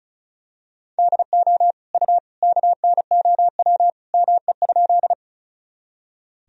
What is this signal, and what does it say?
Morse code, heard as a keyed tone, about 34 wpm, 700 Hz. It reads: DO U KNOW ME?
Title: Sentences from Top 200 Words and Abbreviations 35wpm